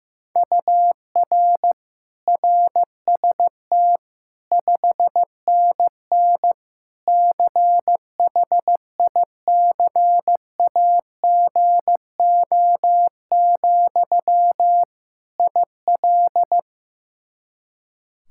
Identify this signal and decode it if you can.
Morse code, heard as a keyed tone, about 16 wpm, 700 Hz. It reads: UR RST 5NN CHICAGO, IL
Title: K1USN SST QSO Practice 15wpm